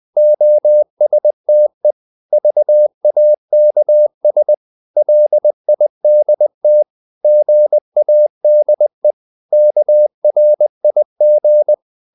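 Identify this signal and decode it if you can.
Morse code, heard as a keyed tone, about 20 wpm, 600 Hz. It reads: OSTE VAKS LIDT GADE KRIG
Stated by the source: Morse code practice of short Danish words at 20 wpm, from CW med Gnister.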